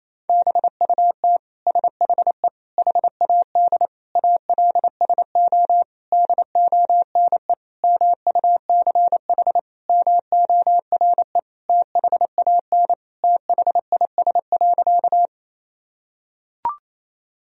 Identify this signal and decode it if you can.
Morse code, heard as a keyed tone, about 28 wpm, 700 Hz. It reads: BUT SHE HAD ALSO DONE MUCH MORE THAN THIS.
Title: Sentences from Top 1000 Words - Review Part 1 - 28wpm